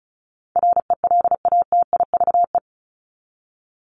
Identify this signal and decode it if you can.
Morse code, heard as a keyed tone, about 34 wpm, 700 Hz. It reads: RELATIVE
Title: LY Suffix - Top 100 Words with Base Words 35wpm